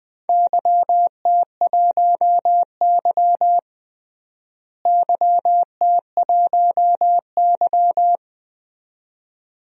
Morse code, heard as a keyed tone, about 20 wpm, 700 Hz. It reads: YT1Y YT1Y
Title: DXCC-100 Call Signs - 4 Characters - Repeated 2x - 20wpm